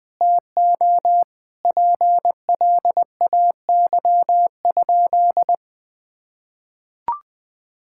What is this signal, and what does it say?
Morse code, heard as a keyed tone, about 20 wpm, 700 Hz. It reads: TO PLAY?
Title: Sentences from Top 1000 Words - Review Part 6 - 20wpm